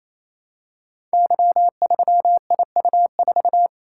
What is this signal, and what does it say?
Morse code, heard as a keyed tone, about 27 wpm, 700 Hz. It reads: Y3IU4
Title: Code Groups - 5 Letter:Numbers - 28wpm